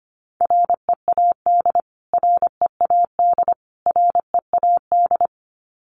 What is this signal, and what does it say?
Morse code, heard as a keyed tone, about 25 wpm, 700 Hz. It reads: READ READ READ
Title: Top 500 Words Repeated 25wpm